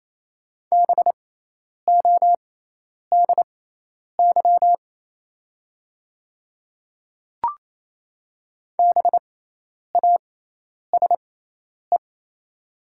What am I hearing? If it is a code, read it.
Morse code, heard as a keyed tone, about 28 wpm, spaced as for 10 wpm, 690 Hz. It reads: BODY BASE
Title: 6x Character Spacing - Top 500 Words - 28wpm